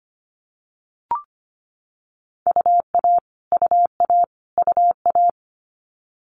Morse code, heard as a keyed tone, about 25 wpm, 700 Hz. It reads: UA UA UA
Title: Multi-letter Phonemes Repeated 3x 25wpm